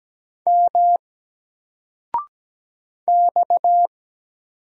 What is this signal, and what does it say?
Morse code, heard as a keyed tone, about 17 wpm, 700 Hz. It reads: M X
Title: Single Letter-Number 17wpm